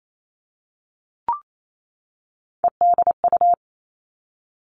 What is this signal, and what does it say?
Morse code, heard as a keyed tone, about 28 wpm, 700 Hz. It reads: EDU
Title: Three Challenging Letter-Number Combinations 28wpm